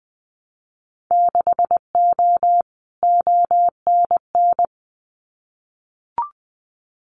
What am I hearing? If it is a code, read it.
Morse code, heard as a keyed tone, about 20 wpm, 700 Hz. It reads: 6O ONN
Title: Winter Field Day Exchanges Encore 20wpm